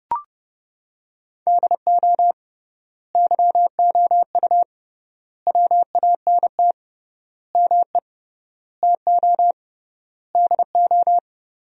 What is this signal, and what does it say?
Morse code, heard as a keyed tone, about 32 wpm, 700 Hz. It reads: DO YOU WANT ME TO DO
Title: 3x Word-Spacing - Sentences from Top 200 Words - 30wpm